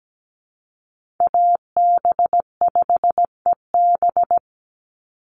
Morse code, heard as a keyed tone, about 17 wpm, 700 Hz. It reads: AB5EB
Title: Call Signs - 17wpm